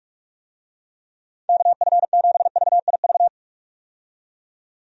Morse code, heard as a keyed tone, about 41 wpm, 700 Hz. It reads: KF7VIV
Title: USA Call Signs - 6 Characters - 45wpm